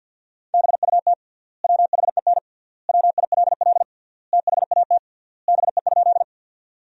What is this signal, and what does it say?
Morse code, heard as a keyed tone, about 50 wpm, 700 Hz. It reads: BUT WHEN WILL THAT BE?
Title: 3x Word-Spacing - Sentences from Top 100 Words - 50wpm